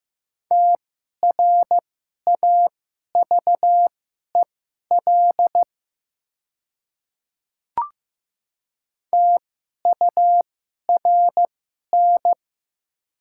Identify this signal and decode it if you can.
Morse code, heard as a keyed tone, about 15 wpm, spaced as for 11 wpm, 690 Hz. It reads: TRAVEL TURN
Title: 2x Character Spacing - Top 500 Words - 15wpm